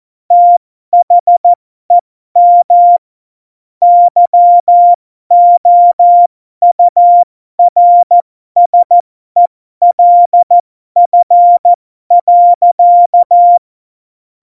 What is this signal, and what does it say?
Morse code, heard as a keyed tone, about 14 wpm, spaced as for 12 wpm, 700 Hz. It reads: THEM YOURSELF.
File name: Morsekurs / Witze